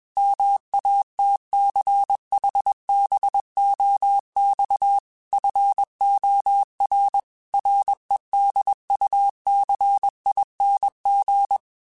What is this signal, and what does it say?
Morse code, heard as a keyed tone, about 21 wpm, 800 Hz. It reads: MATCHBOX FOR REDUCING
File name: QSOs / QSOs-mp3s / WD0FBT 1@20